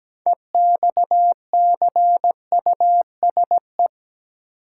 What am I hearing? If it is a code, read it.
Morse code, heard as a keyed tone, about 17 wpm, 700 Hz. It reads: EXCUSE
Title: Sentences from Top 700 Words - New Word Drill - 17wpm